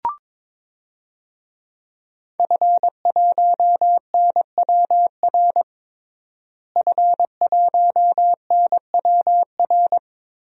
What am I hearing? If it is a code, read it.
Morse code, heard as a keyed tone, about 21 wpm, 700 Hz. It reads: F1NWR F1NWR
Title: DXCC-100 Call Signs - 5 Characters - Repeated 2x - 22wpm